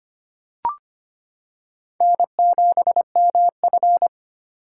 Morse code, heard as a keyed tone, about 25 wpm, 700 Hz. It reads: N7MF
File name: practice / 25wpm / Four Letter-Number Combinations - 25wpm